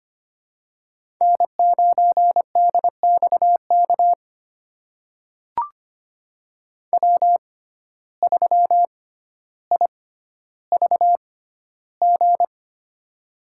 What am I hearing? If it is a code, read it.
Morse code, heard as a keyed tone, about 25 wpm, spaced as for 13 wpm, 700 Hz. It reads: N9DXK W3IVG
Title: USA Call Signs - 5 Characters - 6x Character Spacing - 25wpm